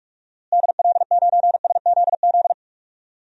Morse code, heard as a keyed tone, about 47 wpm, 700 Hz. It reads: DL9SZZ
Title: DXCC-100 Call Signs - 6 Characters - Repeated 2x - 45wpm